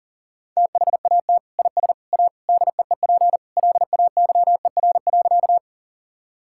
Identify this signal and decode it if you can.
Morse code, heard as a keyed tone, about 37 wpm, 700 Hz. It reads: THAT IS A DEEP LAYER.
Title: Sentences from Top 1600 Words 40wpm